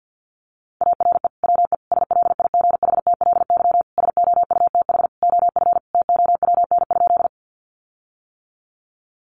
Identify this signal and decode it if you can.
Morse code, heard as a keyed tone, about 50 wpm, 700 Hz. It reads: ARE WE SLIGHTLY SOUTH OF TOWN?
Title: Sentences from Top 1300 Words 50wpm